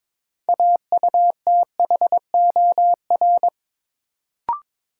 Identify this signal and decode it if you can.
Morse code, heard as a keyed tone, about 22 wpm, 700 Hz. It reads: AUTHOR
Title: Sets of 4 Words - New Word Drill - 22wpm